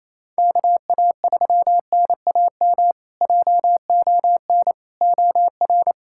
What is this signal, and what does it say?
Morse code, heard as a keyed tone, about 28 wpm, 700 Hz. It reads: KA3NAM JON OR
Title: K1USN SST QSO Practice - 28wpm